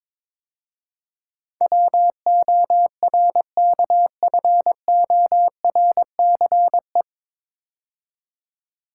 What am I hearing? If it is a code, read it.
Morse code, heard as a keyed tone, about 22 wpm, 700 Hz. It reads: WORKFORCE
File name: 9 Letter Words - First 3 Letters Given - 22wpm